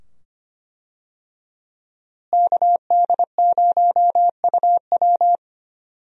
Morse code, heard as a keyed tone, about 25 wpm, 700 Hz. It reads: KD0UW
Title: USA Call Signs - 5 Characters - Repeated 2x - 25wpm